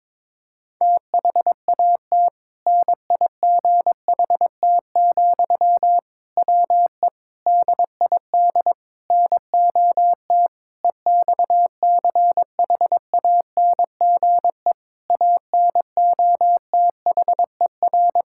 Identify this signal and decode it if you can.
Morse code, heard as a keyed tone, about 22 wpm, 700 Hz. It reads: THAT NIGHT, WE DID NOT EXCHANGE ANOTHER
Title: Sentences from Top 1000 Words - Review Part 6 - 22wpm